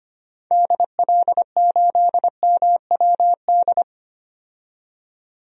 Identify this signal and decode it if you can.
Morse code, heard as a keyed tone, about 25 wpm, 700 Hz. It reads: DL8MWD
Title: DXCC-100 Call Signs - 6 Characters - 25wpm